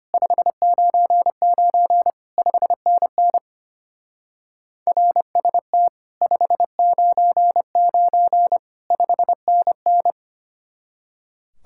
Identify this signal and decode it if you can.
Morse code, heard as a keyed tone, about 27 wpm, 700 Hz. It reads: 599 5NN RST 599 5NN
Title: Speed-Racing Common QSO Phrases - 25wpm